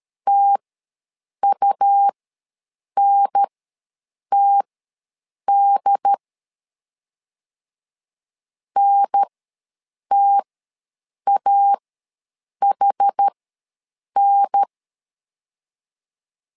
Morse code, heard as a keyed tone, about 13 wpm, spaced as for 7 wpm, 800 Hz. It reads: TUNTD NTAHN
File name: LEZIONI / 2 UHDN / 2 UHDN ws 12wpm 1708